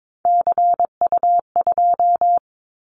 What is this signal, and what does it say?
Morse code, heard as a keyed tone, about 22 wpm, 700 Hz. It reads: CU2
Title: Callsign Prefixes for Top 50 DXCC - 22wpm